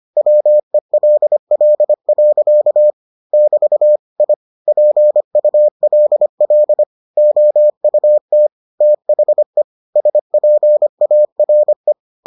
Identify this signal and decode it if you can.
Morse code, heard as a keyed tone, about 25 wpm, 600 Hz. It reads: WELL. <BT> I PULL OUT THE SPARE